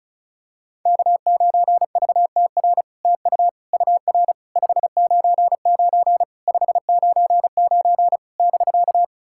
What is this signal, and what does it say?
Morse code, heard as a keyed tone, about 36 wpm, 700 Hz. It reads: K9VTR TU UR 599 599 <BK>
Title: SOTA QSO Practice 35wpm